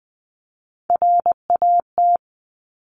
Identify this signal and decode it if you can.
Morse code, heard as a keyed tone, about 20 wpm, 700 Hz. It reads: RAT
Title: Top Three Letter Combinations in Words 20wpm